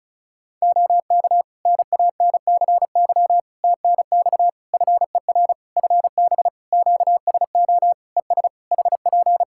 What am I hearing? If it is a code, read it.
Morse code, heard as a keyed tone, about 36 wpm, 700 Hz. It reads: OK NANCY TNX FER FB QSO ES HP
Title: Practice QSO - Exchange 4 - Ending - 35wpm